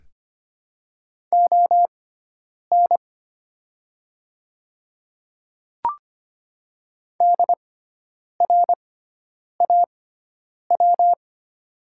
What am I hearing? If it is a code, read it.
Morse code, heard as a keyed tone, about 25 wpm, spaced as for 9 wpm, 700 Hz. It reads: ON DRAW
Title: 6x Character Spacing - Top 500 Words - 25wpm